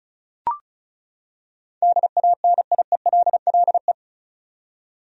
Morse code, heard as a keyed tone, about 36 wpm, 700 Hz. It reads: DANIELLE
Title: USA Common Names - 35wpm